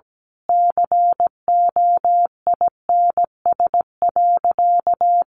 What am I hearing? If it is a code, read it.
Morse code, heard as a keyed tone, about 17 wpm, 700 Hz. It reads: COINS.
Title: A Sound Of Thunder 17wpm